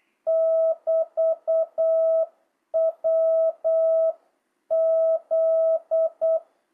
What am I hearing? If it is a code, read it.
Morse code, heard as a keyed tone, about 8 wpm, 660 Hz. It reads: <BT> WZ